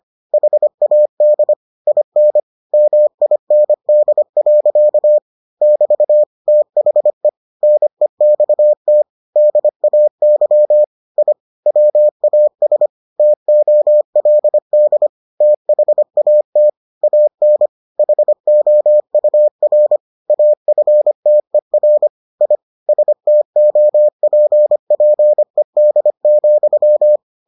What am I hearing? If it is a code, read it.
Morse code, heard as a keyed tone, about 25 wpm, 600 Hz. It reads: HAD IN MIND. <BT> THE NEXT DAY I WAS TOLD THAT AN HOUR AFTER I STOPPED,